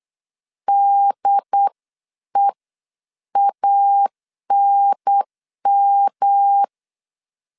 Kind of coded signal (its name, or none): Morse code